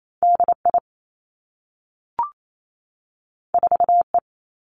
Morse code, heard as a keyed tone, about 28 wpm, 700 Hz. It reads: D I 4 E
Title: Two Challenging Letter-Number Combinations 28wpm